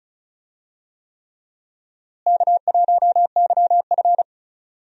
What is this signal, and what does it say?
Morse code, heard as a keyed tone, about 36 wpm, 700 Hz. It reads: K1YF
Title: USA Call Signs - 4 Characters - Repeated 3x - 35wpm